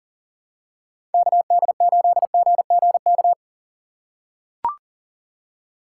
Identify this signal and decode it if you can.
Morse code, heard as a keyed tone, about 40 wpm, 700 Hz. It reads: KD8GGK
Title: USA Call Signs - 6 Characters - 4x Character Spacing - 40wpm